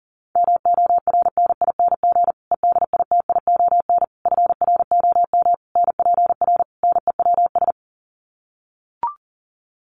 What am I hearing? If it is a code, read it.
Morse code, heard as a keyed tone, about 40 wpm, 700 Hz. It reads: MORNING EDITION FROM NPR NEWS
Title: Sets of 5 Words Encore 40wpm